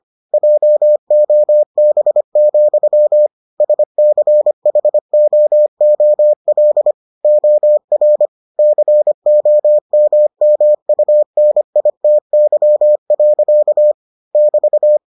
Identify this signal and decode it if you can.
Morse code, heard as a keyed tone, about 25 wpm, 600 Hz. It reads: JOB, SCHOOL OR COMMUNITY. <BT>